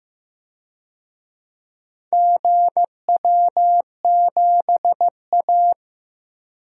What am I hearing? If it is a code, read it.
Morse code, heard as a keyed tone, about 15 wpm, 700 Hz. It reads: GW7A
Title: DXCC-100 Call Signs - 4 Characters - 15wpm